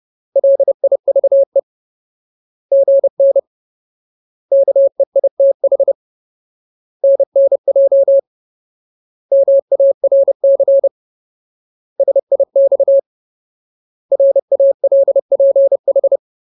Morse code, heard as a keyed tone, about 30 wpm, 550 Hz. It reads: LIVE GN KEITH NNJ MARC SIX RALPH
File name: Ultimate Word List / 30WPM Ultimate